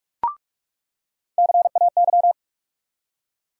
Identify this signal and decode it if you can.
Morse code, heard as a keyed tone, about 46 wpm, 700 Hz. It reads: CAY
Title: USA Call Sign Postfixes - 3 Letters 45wpm